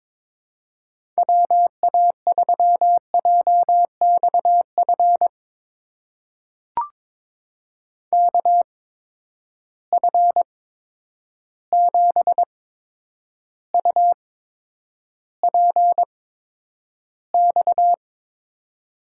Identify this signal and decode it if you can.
Morse code, heard as a keyed tone, about 22 wpm, spaced as for 9 wpm, 700 Hz. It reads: WA3JXF KF7UPX
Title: USA Call Signs - 6 Characters - 8x Character Spacing - 22wpm